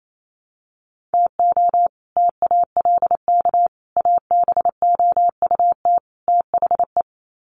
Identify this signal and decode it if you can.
Morse code, heard as a keyed tone, about 28 wpm, 700 Hz. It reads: TO TALK ABOUT THE
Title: Sets of 4 Words 28wpm